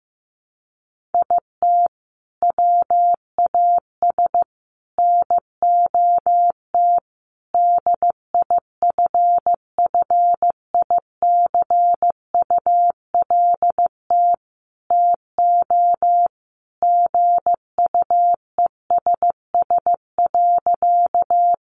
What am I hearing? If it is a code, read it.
Morse code, heard as a keyed tone, about 15 wpm, 700 Hz. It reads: IT WAS NOT DIFFICULT TO GUESS.
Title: Sentences from Top 1500 Words 15wpm